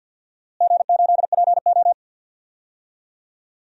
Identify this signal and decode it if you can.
Morse code, heard as a keyed tone, about 47 wpm, 700 Hz. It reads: G8PO
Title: DXCC-100 Call Signs - 4 Characters - Repeated 3x - 50wpm